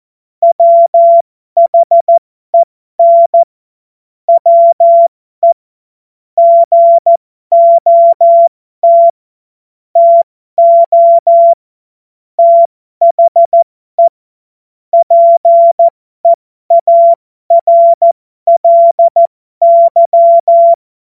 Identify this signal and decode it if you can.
Morse code, heard as a keyed tone, about 14 wpm, spaced as for 12 wpm, 680 Hz. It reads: WHEN WE GOT TO THE PEARLY